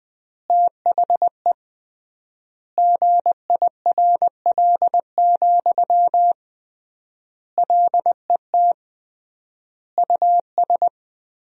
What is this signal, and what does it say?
Morse code, heard as a keyed tone, about 21 wpm, 700 Hz. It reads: THE GIRL, LET US
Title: 3x Word-Spacing - Sentences from Top 400 Words - 20wpm